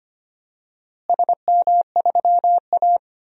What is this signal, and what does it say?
Morse code, heard as a keyed tone, about 25 wpm, 700 Hz. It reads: SM3A
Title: DXCC-100 Call Signs - 4 Characters - Repeated 3x - 25wpm